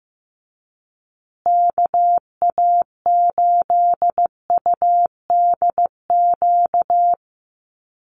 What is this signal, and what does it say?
Morse code, heard as a keyed tone, about 15 wpm, 700 Hz. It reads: KA8UDQ
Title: USA Call Signs Repeated 3x 15wpm